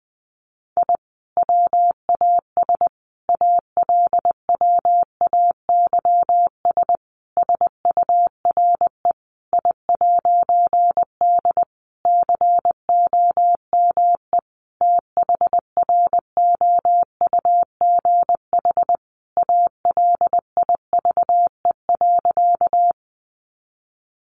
Morse code, heard as a keyed tone, about 20 wpm, 700 Hz. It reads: I WAS ALWAYS SURE I'D COME THROUGH ALIVE.
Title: A Sound Of Thunder 20wpm